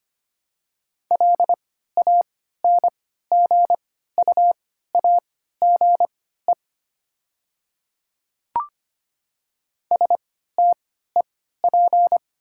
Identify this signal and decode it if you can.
Morse code, heard as a keyed tone, about 25 wpm, spaced as for 14 wpm, 700 Hz. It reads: LANGUAGE STEP
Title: 3x Character Spacing - Top 500 Words - 25wpm